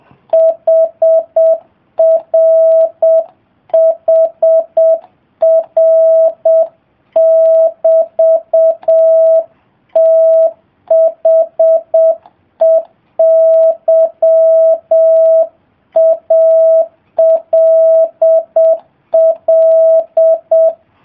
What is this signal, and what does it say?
Morse code, heard as a keyed tone, about 7 wpm, 650 Hz. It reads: HRHR<BT>THEY ALL